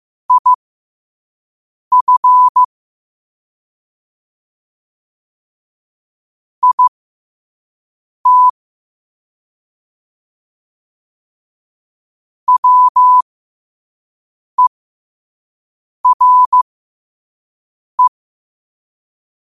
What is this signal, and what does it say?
Morse code, heard as a keyed tone, about 15 wpm, spaced as for 5 wpm, 1000 Hz. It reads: IF IT WERE
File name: qotdfiles / 5wpm